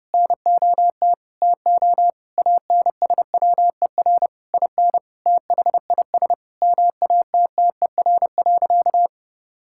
Morse code, heard as a keyed tone, about 30 wpm, 700 Hz. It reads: NOT TO ANSWER IN THIS MATTER.
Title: Sentences from Top 1600 Words 30wpm